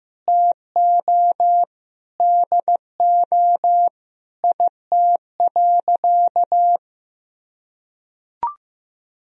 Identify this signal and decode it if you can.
Morse code, heard as a keyed tone, about 15 wpm, 700 Hz. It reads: TO DO IT.
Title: Sentences from Top 1000 Words - Review Part 2 - 15wpm